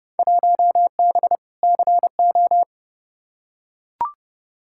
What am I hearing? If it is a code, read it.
Morse code, heard as a keyed tone, about 30 wpm, 700 Hz. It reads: 1B CO
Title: Field Day Exchanges - 30wpm